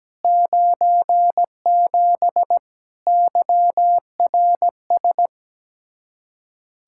Morse code, heard as a keyed tone, about 17 wpm, 700 Hz. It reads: 97 YRS
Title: Practice QSO - Exchange 3 - More Info - 17wpm